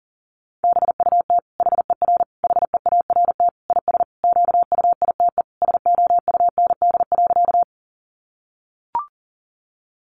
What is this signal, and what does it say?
Morse code, heard as a keyed tone, about 40 wpm, 700 Hz. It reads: BUT HER HEART IS QUITE SOUND.
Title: Sentences from Top 1200 Words 40wpm